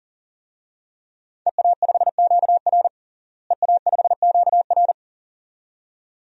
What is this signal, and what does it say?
Morse code, heard as a keyed tone, about 40 wpm, 700 Hz. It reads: EA5QR EA5QR
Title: DXCC-100 Call Signs - 5 Characters - Repeated 2x - 40wpm